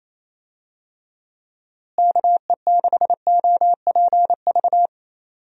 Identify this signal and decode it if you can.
Morse code, heard as a keyed tone, about 28 wpm, 700 Hz. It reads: KE6OPV
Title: USA Call Signs - 6 Characters - Repeated 2x - 28wpm